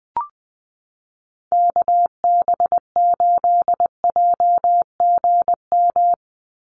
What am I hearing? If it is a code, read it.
Morse code, heard as a keyed tone, about 20 wpm, 700 Hz. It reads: KB8JGM
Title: USA Call Signs 20wpm